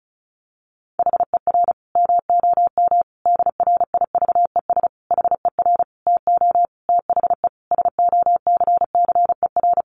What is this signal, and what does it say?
Morse code, heard as a keyed tone, about 34 wpm, 700 Hz. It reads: HER MOM DRIVES HER TO THE SOCCER
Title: Easy Phrases 35wpm